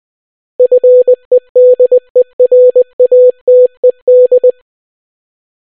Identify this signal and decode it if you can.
Morse code, heard as a keyed tone, about 20 wpm, 500 Hz. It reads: FEDERATED